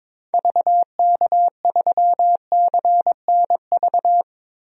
Morse code, heard as a keyed tone, about 22 wpm, 700 Hz. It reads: VK3CNV
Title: DXCC-100 Call Signs - 6 Characters - 22wpm